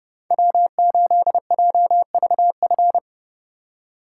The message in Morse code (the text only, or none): W8JVF